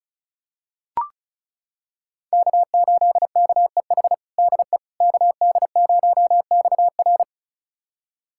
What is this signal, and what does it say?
Morse code, heard as a keyed tone, about 34 wpm, 700 Hz. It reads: K8KEH DE KD0XR
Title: USA Callsign DE Callsign - 5 Characters - 35wpm